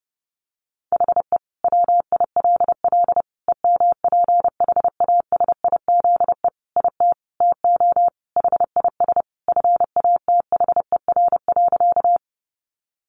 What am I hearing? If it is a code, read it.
Morse code, heard as a keyed tone, about 30 wpm, 700 Hz. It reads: HE WILL EMPHASIZE IT TO HIS FATHER.
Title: Sentences from Top 1500 Words 30wpm